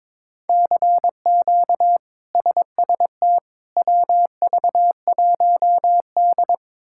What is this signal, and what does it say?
Morse code, heard as a keyed tone, about 22 wpm, 700 Hz. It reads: CQ SST WV1D
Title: K1USN SST QSO Practice - 22wpm